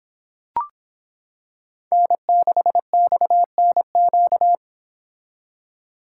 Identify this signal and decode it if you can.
Morse code, heard as a keyed tone, about 26 wpm, 700 Hz. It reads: N6XNQ